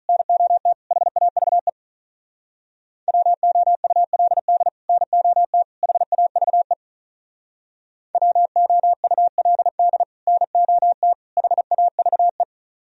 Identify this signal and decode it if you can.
Morse code, heard as a keyed tone, about 46 wpm, 700 Hz. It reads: NOT HAVE WOULD NOT HAVE WOULD NOT HAVE